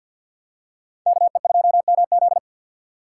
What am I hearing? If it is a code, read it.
Morse code, heard as a keyed tone, about 50 wpm, 700 Hz. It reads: KE2MZ
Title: USA Call Signs - 4x Character Spacing - 50wpm